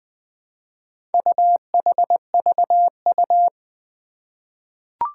Morse code, heard as a keyed tone, about 20 wpm, 700 Hz. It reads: UHVU
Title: Four Challenging Letter-Number Combinations 20wpm